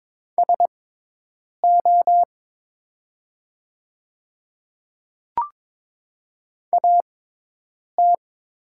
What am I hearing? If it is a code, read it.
Morse code, heard as a keyed tone, about 22 wpm, spaced as for 8 wpm, 700 Hz. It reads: SO AT
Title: 6x Character Spacing - Top 100 Words - 22wpm